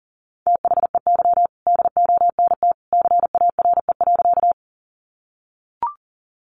Morse code, heard as a keyed tone, about 40 wpm, 700 Hz. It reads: THEY DONT CARE.
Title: Sentences from Top 300 Words 40wpm